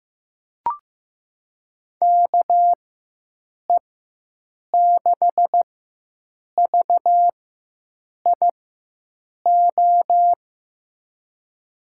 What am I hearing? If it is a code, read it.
Morse code, heard as a keyed tone, about 15 wpm, spaced as for 7 wpm, 700 Hz. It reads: KE6VIO
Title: USA Call Signs - 4x Character Spacing - 15wpm